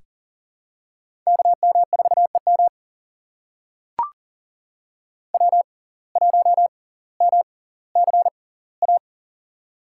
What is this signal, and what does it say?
Morse code, heard as a keyed tone, about 40 wpm, spaced as for 20 wpm, 700 Hz. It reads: KM4EM W1MCA
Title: USA Call Signs - 6x Character Spacing - 40wpm